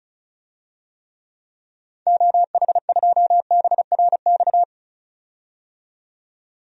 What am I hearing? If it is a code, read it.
Morse code, heard as a keyed tone, about 36 wpm, 700 Hz. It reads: OH2BRX
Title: DXCC-100 Call Signs - 6 Characters - 35wpm